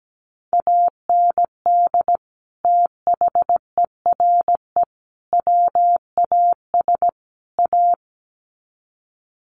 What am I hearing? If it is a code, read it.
Morse code, heard as a keyed tone, about 17 wpm, 700 Hz. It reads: AND THERE WAS A
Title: Sets of 4 Words 17wpm